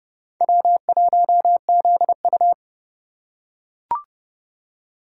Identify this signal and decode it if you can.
Morse code, heard as a keyed tone, about 28 wpm, 700 Hz. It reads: W1ZU